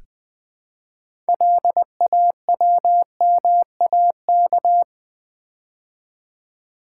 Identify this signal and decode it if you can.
Morse code, heard as a keyed tone, about 20 wpm, 700 Hz. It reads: LAWMAK
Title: Build A Word - 2 Letters At A Time - 8-Letter Words - 20wpm